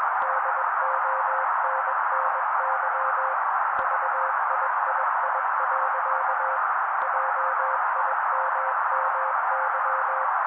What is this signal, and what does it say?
Morse code, heard as a keyed tone, about 21 wpm, 530 Hz. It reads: DONNY VIII. JIMMY
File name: daily / 20wpm noise